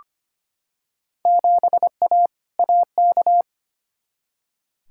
Morse code, heard as a keyed tone, about 25 wpm, 700 Hz. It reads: 7A AK
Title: Field Day Exchanges Encore - 25wpm